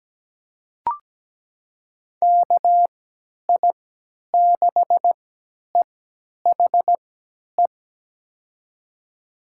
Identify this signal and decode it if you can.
Morse code, heard as a keyed tone, about 17 wpm, spaced as for 10 wpm, 700 Hz. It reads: KI6EHE
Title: USA Call Signs - 6 Characters - 3x Character Spacing - 17wpm